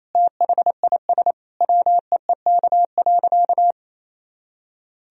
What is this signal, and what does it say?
Morse code, heard as a keyed tone, about 28 wpm, 700 Hz. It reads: THIS WEEK.
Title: Sentences from Top 1000 Words - Review Part 2 - 28wpm